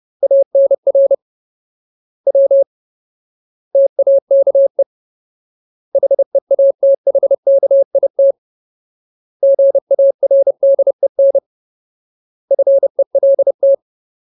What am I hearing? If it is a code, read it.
Morse code, heard as a keyed tone, about 30 wpm, 550 Hz. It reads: ANR W TAKE HEATHKIT GARDEN FELT
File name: Ultimate Word List / 30WPM Ultimate